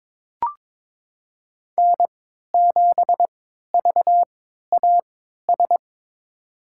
Morse code, heard as a keyed tone, about 22 wpm, spaced as for 12 wpm, 700 Hz. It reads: N7VAS
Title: USA Call Signs - 5 Characters - 3x Character Spacing - 22wpm